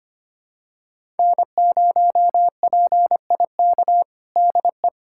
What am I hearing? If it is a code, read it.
Morse code, heard as a keyed tone, about 25 wpm, 700 Hz. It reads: N0PIK DE
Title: USA Callsign DE Callsign 25wpm